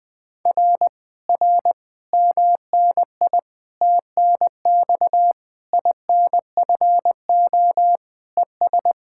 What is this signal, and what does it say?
Morse code, heard as a keyed tone, about 20 wpm, 700 Hz. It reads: R R MNI TNX INFO ES
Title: Speed-Racing Common QSO Phrases - 17wpm